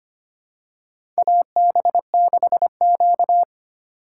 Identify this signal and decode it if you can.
Morse code, heard as a keyed tone, about 25 wpm, 700 Hz. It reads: AB6Q
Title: USA Call Signs - 4x Character Spacing - 25wpm